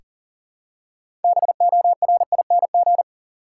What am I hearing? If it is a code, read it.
Morse code, heard as a keyed tone, about 40 wpm, 700 Hz. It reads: BORING
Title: Six Letter Words - 40wpm